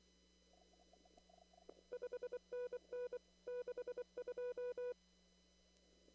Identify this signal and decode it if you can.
Morse code, heard as a keyed tone, about 24 wpm, 490 Hz. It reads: 5NN 62
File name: CW / kt5j wpx cw-op-k5pi